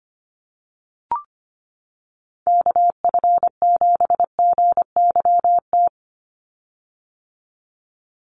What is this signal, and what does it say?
Morse code, heard as a keyed tone, about 25 wpm, 690 Hz. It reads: KF7GYT
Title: USA Call Signs 25wpm